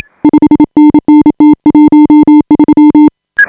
Morse code, heard as a keyed tone, about 28 wpm, 300 Hz. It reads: 5NN T13
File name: AUDIO2018 / HA5BA CW